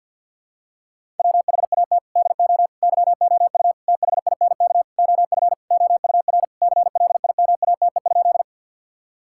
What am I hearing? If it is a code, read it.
Morse code, heard as a keyed tone, about 50 wpm, 700 Hz. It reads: WHAT DO YOU THINK OF OUR CLIMATE?
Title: Sentences from Top 1600 Words 50wpm